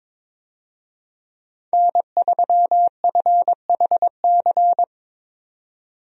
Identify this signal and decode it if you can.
Morse code, heard as a keyed tone, about 22 wpm, 700 Hz. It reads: N3FHC
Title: USA Call Signs - 5 Characters - 22wpm